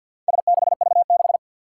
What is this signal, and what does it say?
Morse code, heard as a keyed tone, about 53 wpm, 700 Hz. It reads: S6V6
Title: Four Challenging Letter-Number Combinations 50wpm